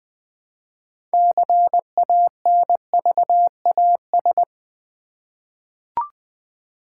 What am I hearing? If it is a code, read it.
Morse code, heard as a keyed tone, about 20 wpm, 700 Hz. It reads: CANVAS
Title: Easy Words with Hints - 20wpm